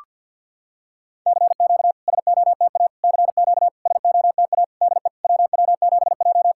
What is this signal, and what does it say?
Morse code, heard as a keyed tone, about 51 wpm, 690 Hz. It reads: CQ SOTA CQ SOTA DE WW7J
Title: SOTA QSO Practice 50wpm